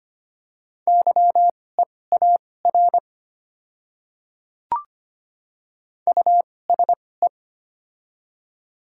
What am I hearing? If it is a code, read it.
Morse code, heard as a keyed tone, about 25 wpm, spaced as for 18 wpm, 700 Hz. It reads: YEAR USE
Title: 2x Character Spacing - Top 500 Words - 25wpm